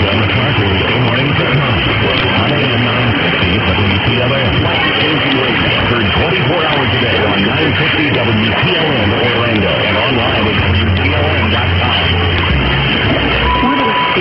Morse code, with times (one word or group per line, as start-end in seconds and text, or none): none